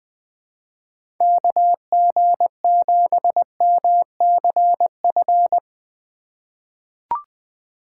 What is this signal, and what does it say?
Morse code, heard as a keyed tone, about 20 wpm, 700 Hz. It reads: KG7MCF
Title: USA Call Signs - 6 Characters - 4x Character Spacing - 20wpm